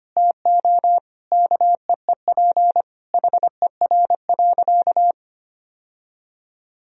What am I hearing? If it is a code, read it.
Morse code, heard as a keyed tone, about 25 wpm, 700 Hz. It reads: TO KEEP HER.
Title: Sentences from Top 1000 Words - Review Part 2 - 25wpm